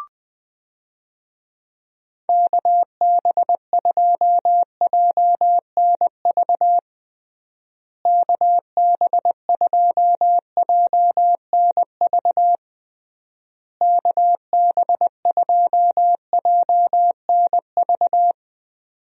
Morse code, heard as a keyed tone, about 21 wpm, 700 Hz. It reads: KB2JNV KB2JNV KB2JNV
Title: USA Call Signs - 6 Characters - Repeated 3x - 20wpm